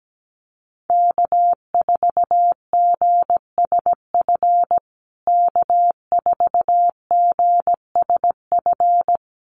Morse code, heard as a keyed tone, about 17 wpm, 700 Hz. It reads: K4GSF K4GSF
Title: USA Call Signs Repeated 2x 17wpm